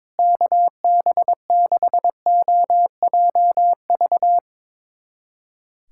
Morse code, heard as a keyed tone, about 22 wpm, 700 Hz. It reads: KB6OJV
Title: USA Callsign DE Callsign - 6 Characters - 22wpm